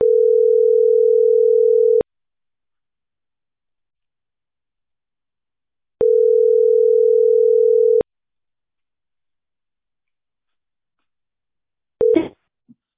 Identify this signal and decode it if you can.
Ringback tone (North America), 440 + 480 Hz.